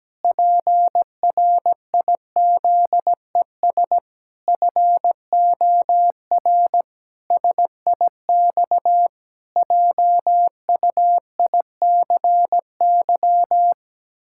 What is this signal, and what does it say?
Morse code, heard as a keyed tone, about 17 wpm, 700 Hz. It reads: PRIZES FOR SIX JUICY
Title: Pangram Challenge v2 - 17wpm